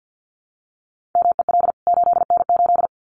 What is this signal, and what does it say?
Morse code, heard as a keyed tone, about 50 wpm, 700 Hz. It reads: MEL 8N8
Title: CWT Exchange 50wpm